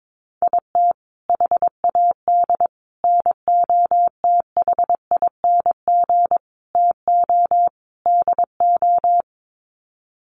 Morse code, heard as a keyed tone, about 22 wpm, 700 Hz. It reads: IT HAD NOTHING TO DO
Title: Sets of 5 Words Encore2 22wpm